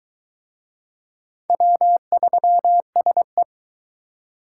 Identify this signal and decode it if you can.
Morse code, heard as a keyed tone, about 22 wpm, 700 Hz. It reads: W3SE